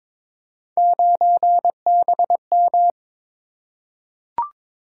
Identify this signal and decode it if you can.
Morse code, heard as a keyed tone, about 22 wpm, 700 Hz. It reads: 9BM
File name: Three Letter-Number Combinations - 22wpm